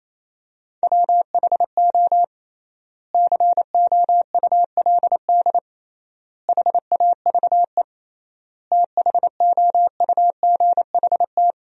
Morse code, heard as a keyed tone, about 28 wpm, 700 Hz. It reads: WHO COULD HAVE THOUGHT
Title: 3x Word-Spacing - Sentences from Top 300 Words - 28wpm